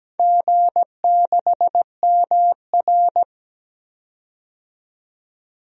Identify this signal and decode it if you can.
Morse code, heard as a keyed tone, about 17 wpm, 700 Hz. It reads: G6MR
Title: DXCC-100 Call Signs - 4 Characters - 17wpm